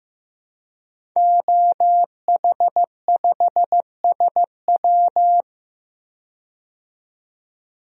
Morse code, heard as a keyed tone, about 15 wpm, 700 Hz. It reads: OH5SW
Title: DXCC-100 Call Signs - 5 Characters - 15wpm